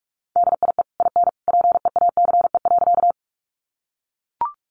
Morse code, heard as a keyed tone, about 44 wpm, 700 Hz. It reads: DIE IN PEACE.
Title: Sentences from Top 1500 Words 45wpm